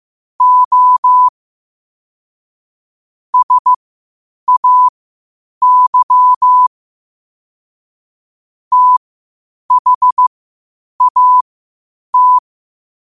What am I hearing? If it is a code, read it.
Morse code, heard as a keyed tone, about 15 wpm, spaced as for 8 wpm, 1000 Hz. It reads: O SAY THAT
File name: qotdfiles / 8wpm